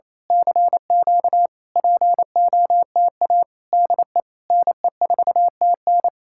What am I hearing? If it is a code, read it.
Morse code, heard as a keyed tone, about 27 wpm, 700 Hz. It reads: CQ POTA DE NE4TN
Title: Speed-Racing Common QSO Phrases Encore - 28wpm